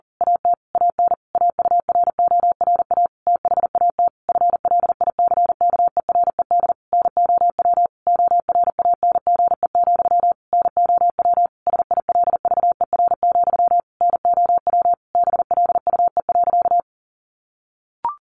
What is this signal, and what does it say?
Morse code, heard as a keyed tone, about 40 wpm, 700 Hz. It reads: AT AN AURORA THAT FLICKERED NOW ORANGE, NOW SILVER, NOW BLUE.
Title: A Sound Of Thunder 40wpm